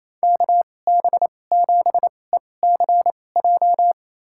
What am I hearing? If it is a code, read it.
Morse code, heard as a keyed tone, about 28 wpm, spaced as for 20 wpm, 700 Hz. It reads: KB7ECJ
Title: USA Call Signs - 6 Characters - 2x Character Spacing - 28wpm